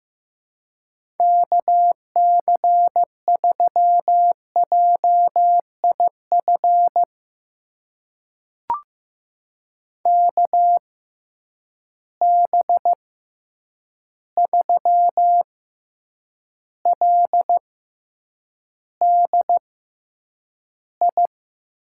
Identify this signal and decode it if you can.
Morse code, heard as a keyed tone, about 15 wpm, spaced as for 8 wpm, 700 Hz. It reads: KC3JIF KB3LDI
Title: USA Call Signs - 6 Characters - 6x Character Spacing - 15wpm